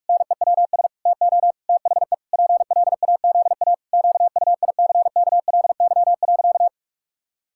Morse code, heard as a keyed tone, about 46 wpm, 700 Hz. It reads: NEWS TO THE PLAZA QUICKLY.
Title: Pangram Challenge v2 - 45wpm